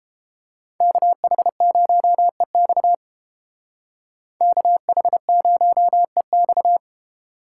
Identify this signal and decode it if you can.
Morse code, heard as a keyed tone, about 31 wpm, 700 Hz. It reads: KH0EX KH0EX